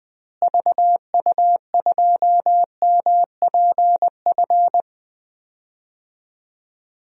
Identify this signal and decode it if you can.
Morse code, heard as a keyed tone, about 20 wpm, 700 Hz. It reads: VU2MPF
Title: DXCC-100 Call Signs - 6 Characters - Repeated 2x - 20wpm